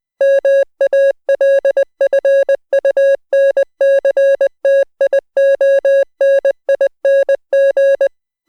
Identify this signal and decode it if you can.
Morse code, heard as a keyed tone, about 20 wpm, 550 Hz. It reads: MALFUNCTIONING